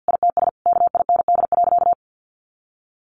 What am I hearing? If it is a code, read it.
Morse code, heard as a keyed tone, about 50 wpm, 700 Hz. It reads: ITS KIND.
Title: Sentences from Top 1000 Words 50wpm